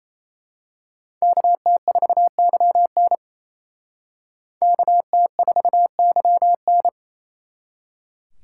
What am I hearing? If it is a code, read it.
Morse code, heard as a keyed tone, about 29 wpm, 700 Hz. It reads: KT4YN KT4YN
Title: Speed Racing - USA 2x2 Call Signs 28wpm